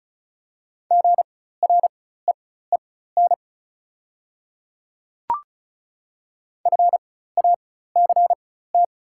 Morse code, heard as a keyed tone, about 34 wpm, spaced as for 16 wpm, 690 Hz. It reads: GREEN FACT
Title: 4x Character Spacing - Top 500 Words - 35wpm